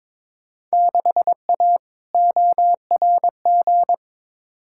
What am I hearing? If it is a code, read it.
Morse code, heard as a keyed tone, about 22 wpm, 700 Hz. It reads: 6A ORG
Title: Field Day Exchanges Encore - 22wpm